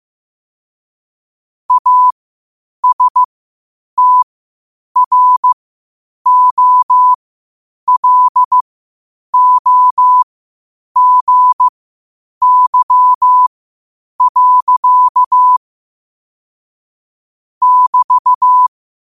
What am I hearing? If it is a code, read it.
Morse code, heard as a keyed tone, about 15 wpm, spaced as for 8 wpm, 1000 Hz. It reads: ASTROLOGY. <BT>
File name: qotdfiles / 8wpm